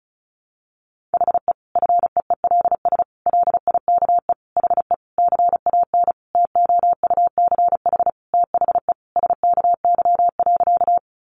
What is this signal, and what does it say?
Morse code, heard as a keyed tone, about 34 wpm, 700 Hz. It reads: HE FEELS LIKE HE CAN TOUCH THE SKY.
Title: Additional Easy Phrases 35wpm